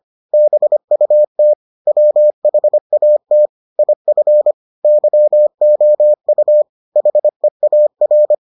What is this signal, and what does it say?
Morse code, heard as a keyed tone, about 25 wpm, 600 Hz. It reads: BUT WHAT IF YOU HEAR